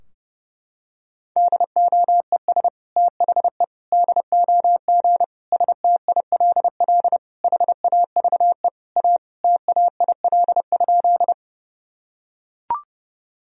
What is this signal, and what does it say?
Morse code, heard as a keyed tone, about 30 wpm, 700 Hz. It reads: DOES THE DOG STILL HAVE A TAIL?
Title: Sentences from Top 1000 Words - Review Part 3 - 30wpm